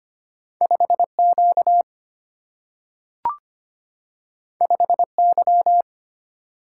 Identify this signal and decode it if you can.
Morse code, heard as a keyed tone, about 25 wpm, 700 Hz. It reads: 5Q 5Y
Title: Single-Letter Single-Number 25wpm